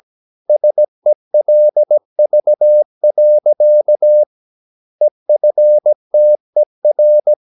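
Morse code, heard as a keyed tone, about 17 wpm, 600 Hz. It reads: SELV. EFTER
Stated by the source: Slow (QRS) Morse practice at 17 wpm, from CW med Gnister.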